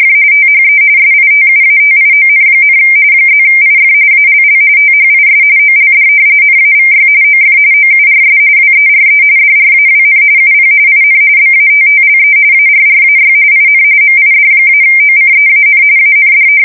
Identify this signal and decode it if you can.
RTTY, about 45.45 baud. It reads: THE SITUATION, +IF ALL THIS OLD NAVY OF OURS WERE DRAWN UP IN BATTLE ARRAY IN MID-OCEAN AND
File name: fsk170 / battleships